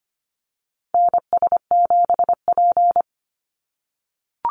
Morse code, heard as a keyed tone, about 25 wpm, 700 Hz. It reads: NS7P
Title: NA SOTA Chaser Callsigns 25wpm